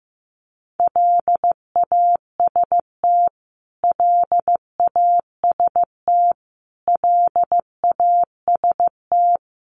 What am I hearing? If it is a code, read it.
Morse code, heard as a keyed tone, about 15 wpm, 700 Hz. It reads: LAST LAST LAST
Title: Top 300 Words Repeated 15wpm